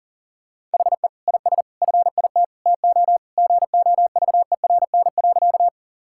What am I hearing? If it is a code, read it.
Morse code, heard as a keyed tone, about 40 wpm, 700 Hz. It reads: HE IS FIT TO GOVERN.
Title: Sentences from Top 1000 Words - Review Part 5 - 40wpm